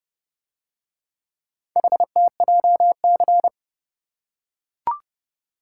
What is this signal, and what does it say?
Morse code, heard as a keyed tone, about 31 wpm, 700 Hz. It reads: HTJC